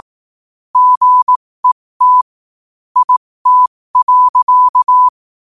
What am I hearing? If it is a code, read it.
Morse code, heard as a keyed tone, about 18 wpm, spaced as for 15 wpm, 1000 Hz. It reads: GET IT.
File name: qotdfiles / 15wpm